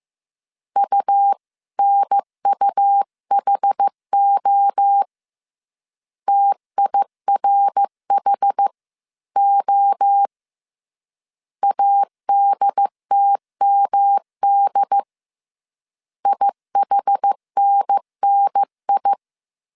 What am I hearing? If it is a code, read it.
Morse code, heard as a keyed tone, about 15 wpm, 800 Hz. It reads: UNUHO TIRHO ADTMD IHNNI